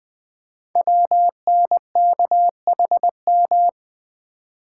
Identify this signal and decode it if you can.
Morse code, heard as a keyed tone, about 20 wpm, 700 Hz. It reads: WNKHM
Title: Code Groups - 5 Letters - 20wpm